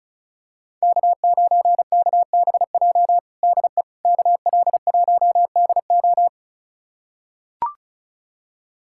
Morse code, heard as a keyed tone, about 34 wpm, 700 Hz. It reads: K9KBJ DE KL1DO
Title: USA Callsign DE Callsign - 5 Characters - 35wpm